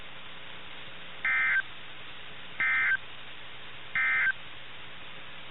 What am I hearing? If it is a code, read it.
Emergency Alert System (SAME) tones, heard as harsh buzzing bursts. It is heard from 1.1 to 4.4 s.
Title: NOAA Weather Radio - KIH61